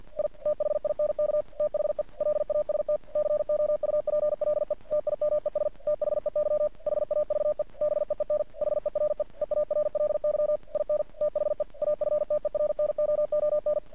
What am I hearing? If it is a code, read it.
Morse code, heard as a keyed tone, about 50 wpm, 600 Hz. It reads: I THINK THE LAST COUPLE TIMES THEY HAVE BEEN HERE EARLY IN THE AFTERNOON
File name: archived / CRSnet